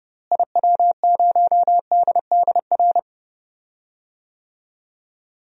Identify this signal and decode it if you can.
Morse code, heard as a keyed tone, about 31 wpm, 700 Hz. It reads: IW0DDR
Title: DXCC-100 Call Signs - 6 Characters - Repeated 3x - 30wpm